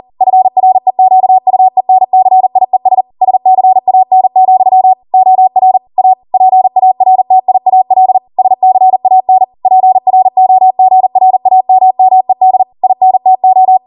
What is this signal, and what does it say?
Morse code, heard as a keyed tone, about 40 wpm, 750 Hz. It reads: FREQUENCIES SCAN, OR A PARTIAL SCAN PROGRAMMED INTO
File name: Morse / 40WPM